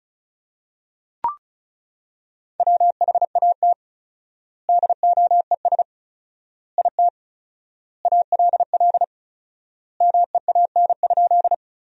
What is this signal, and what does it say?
Morse code, heard as a keyed tone, about 36 wpm, 700 Hz. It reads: WHAT DOES IT ALL MEAN?
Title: 4x Word-Spacing - Sentences from Top 200 Words - 35wpm